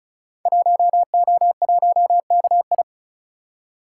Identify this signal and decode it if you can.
Morse code, heard as a keyed tone, about 36 wpm, 700 Hz. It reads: 1O1KI
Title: Code Groups - 5 Letter:Numbers - 35wpm